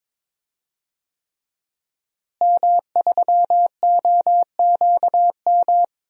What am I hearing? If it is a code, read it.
Morse code, heard as a keyed tone, about 22 wpm, 700 Hz. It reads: M3OQM
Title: DXCC-100 Call Signs - 5 Characters - Repeated 3x - 22wpm